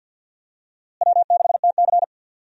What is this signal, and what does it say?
Morse code, heard as a keyed tone, about 48 wpm, 700 Hz. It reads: W6TC
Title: USA Call Signs - 4 Characters - Repeated 2x - 50wpm